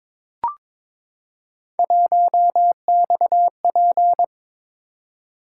Morse code, heard as a keyed tone, about 22 wpm, 700 Hz. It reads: 1XP
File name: practice / 22wpm / Three Letter-Number Combinations - 22wpm